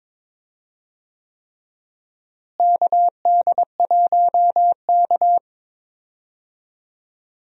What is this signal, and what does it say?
Morse code, heard as a keyed tone, about 22 wpm, 700 Hz. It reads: KD1K
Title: USA Call Signs - 4 Characters - Repeated 3x - 22wpm